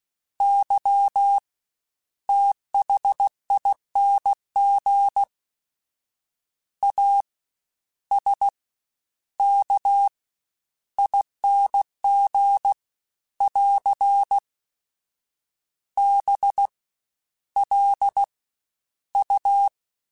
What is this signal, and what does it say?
Morse code, heard as a keyed tone, about 16 wpm, 800 Hz. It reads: Y THING A S K ING <AR> B L U
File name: Special INGs-mp3s / Special INGs-mp3s / SlowToFaster ing @15